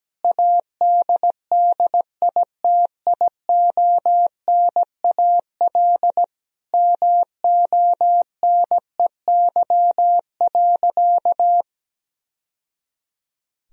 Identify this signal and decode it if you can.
Morse code, heard as a keyed tone, about 17 wpm, 700 Hz. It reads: ADDITIONAL MONEY.
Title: Sentences from Top 1000 Words - Review Part 4 - 17wpm